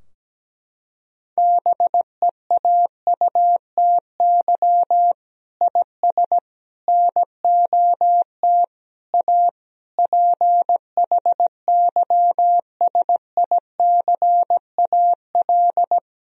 Morse code, heard as a keyed tone, about 17 wpm, 700 Hz. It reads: BEAUTY IS NOT A PHYSICAL
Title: Sentences from Top 1000 Words - Review Part 4 - 17wpm